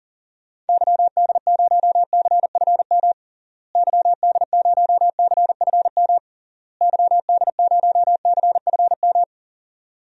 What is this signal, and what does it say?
Morse code, heard as a keyed tone, about 40 wpm, 700 Hz. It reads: YD0CFM YD0CFM YD0CFM
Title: DXCC-100 Call Signs - 6 Characters - Repeated 3x - 40wpm